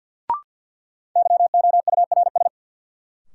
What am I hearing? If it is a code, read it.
Morse code, heard as a keyed tone, about 51 wpm, 700 Hz. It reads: YOURS
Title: Sentences from Top 1500 Words - New Word Drill Alt - 50wpm